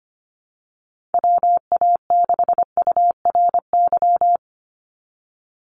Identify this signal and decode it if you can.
Morse code, heard as a keyed tone, about 25 wpm, 700 Hz. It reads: WA6URY
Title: Call Signs - 25wpm